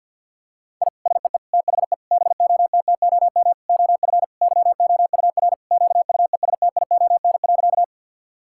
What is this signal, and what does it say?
Morse code, heard as a keyed tone, about 50 wpm, 700 Hz. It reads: I SEE THE BOTTOM OF YOUR QUESTION.
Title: Sentences from Top 1000 Words - Review Part 6 - 50wpm